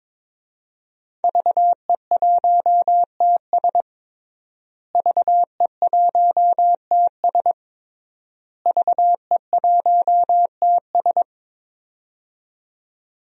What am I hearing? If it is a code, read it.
Morse code, heard as a keyed tone, about 22 wpm, 700 Hz. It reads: VE1TS VE1TS VE1TS
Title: DXCC-100 Call Signs - 5 Characters - Repeated 3x - 22wpm